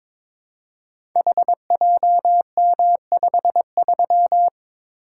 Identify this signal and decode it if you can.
Morse code, heard as a keyed tone, about 22 wpm, 700 Hz. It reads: HJM53
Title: Code Groups - 5 Letter:Numbers - 22wpm